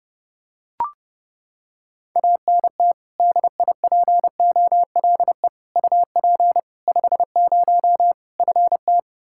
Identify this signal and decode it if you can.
Morse code, heard as a keyed tone, about 30 wpm, 700 Hz. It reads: ANT DIPOLE UP 50 FT
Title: Practice QSO - Exchange 2 - Info - 30wpm